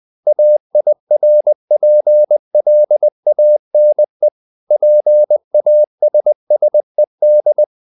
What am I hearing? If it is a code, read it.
Morse code, heard as a keyed tone, about 20 wpm, 600 Hz. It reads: AIRPLANE PASSED
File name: the old man and the sea part 2 20wpm